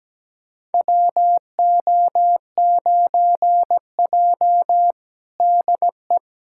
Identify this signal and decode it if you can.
Morse code, heard as a keyed tone, about 17 wpm, 700 Hz. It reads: WO9J DE
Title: USA Callsign DE Callsign - 4 Characters - 17wpm